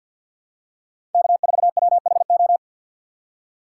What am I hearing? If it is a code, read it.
Morse code, heard as a keyed tone, about 52 wpm, 700 Hz. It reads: K4WHO
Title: USA Call Signs - 4x Character Spacing - 50wpm